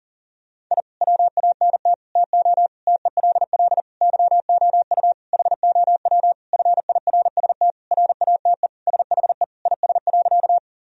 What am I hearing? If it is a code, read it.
Morse code, heard as a keyed tone, about 40 wpm, 700 Hz. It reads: I WANT TO TELL YOU HOW FIRST RATE SHE IS.
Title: Sentences from Top 1000 Words - Review Part 4 - 40wpm